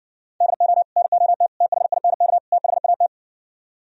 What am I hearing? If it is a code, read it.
Morse code, heard as a keyed tone, about 69 wpm, 700 Hz. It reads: DO NOT THINK THAT